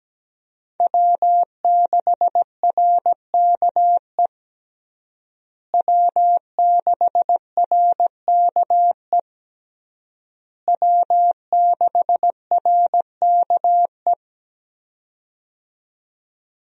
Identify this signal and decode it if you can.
Morse code, heard as a keyed tone, about 17 wpm, 700 Hz. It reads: W6RKE W6RKE W6RKE
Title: USA Call Signs - 5 Characters - Repeated 3x - 17wpm